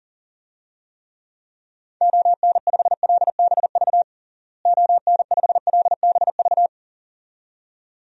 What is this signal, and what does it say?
Morse code, heard as a keyed tone, about 40 wpm, 700 Hz. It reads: ON5LBV ON5LBV
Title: DXCC-100 Call Signs - 6 Characters - Repeated 2x - 40wpm